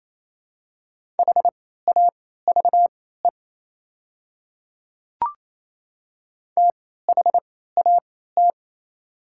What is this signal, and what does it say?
Morse code, heard as a keyed tone, about 28 wpm, spaced as for 16 wpm, 700 Hz. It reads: HAVE THAT
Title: 3x Character Spacing - Top 100 Words - 28wpm